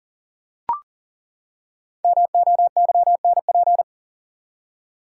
Morse code, heard as a keyed tone, about 37 wpm, 700 Hz. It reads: MOYNP
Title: Code Groups - 5 Letter:Numbers - 40wpm